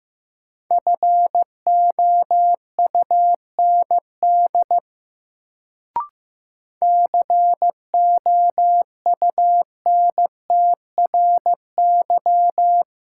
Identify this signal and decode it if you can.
Morse code, heard as a keyed tone, about 15 wpm, 700 Hz. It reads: FOUND COUNTRY
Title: Sentences from Top 300 Words - New Word Drill - 15wpm